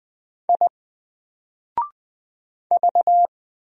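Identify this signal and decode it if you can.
Morse code, heard as a keyed tone, about 20 wpm, 700 Hz. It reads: I V
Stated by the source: Single letter and number Morse drill at 20 wpm, from N4LCD.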